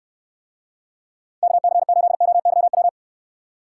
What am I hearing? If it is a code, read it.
Morse code, heard as a keyed tone, about 69 wpm, 700 Hz. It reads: KC8OYW